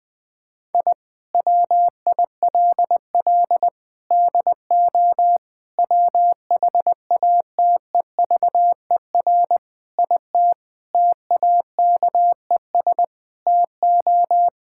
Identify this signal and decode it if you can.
Morse code, heard as a keyed tone, about 20 wpm, 700 Hz. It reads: I WILL DO WHATEVER IT TAKES TO
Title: Sentences with 6-Word Phrases - Spoken Beginning - 20wpm